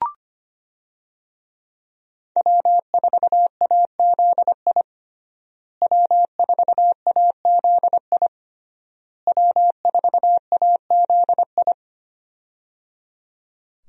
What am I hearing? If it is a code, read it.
Morse code, heard as a keyed tone, about 25 wpm, 700 Hz. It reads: W4AZI W4AZI W4AZI
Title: USA Call Signs - 5 Characters - Repeated 3x - 25wpm